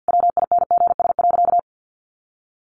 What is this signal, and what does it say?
Morse code, heard as a keyed tone, about 50 wpm, 700 Hz. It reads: WINGS.
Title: Additional Easy Phrases 50wpm